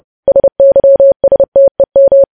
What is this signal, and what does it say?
Morse code, heard as a keyed tone, about 30 wpm, 550 Hz. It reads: SYSTEM
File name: podcast / News headlines-1x-30x30